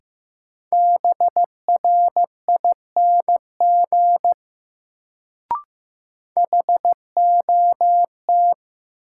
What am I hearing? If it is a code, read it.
Morse code, heard as a keyed tone, about 15 wpm, 700 Hz. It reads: BRING HOT
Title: Sentences from Top 500 Words - New Word Drill - 15wpm